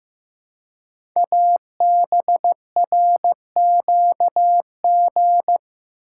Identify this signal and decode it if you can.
Morse code, heard as a keyed tone, about 15 wpm, 700 Hz. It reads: ABRQG
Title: Code Groups - 5 Letters - 15wpm